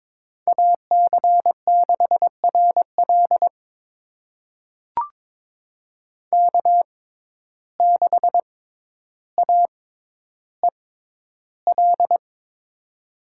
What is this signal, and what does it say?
Morse code, heard as a keyed tone, about 22 wpm, spaced as for 11 wpm, 700 Hz. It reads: AC6RL K6AEL
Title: USA Call Signs - 5 Characters - 6x Character Spacing - 22wpm